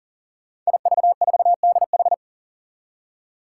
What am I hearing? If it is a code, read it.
Morse code, heard as a keyed tone, about 39 wpm, 700 Hz. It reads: IV4DH
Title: Five Challenging Letter-Number Combinations 40wpm